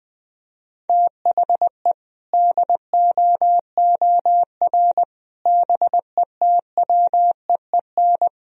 Morse code, heard as a keyed tone, about 20 wpm, 700 Hz. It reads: THE DOOR BETWEEN
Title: Sentences from Top 1000 Words - Review Part 1 - 20wpm